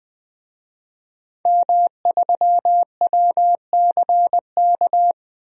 Morse code, heard as a keyed tone, about 20 wpm, 700 Hz. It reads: M3WCK
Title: DXCC-100 Call Signs - 5 Characters - Repeated 2x - 20wpm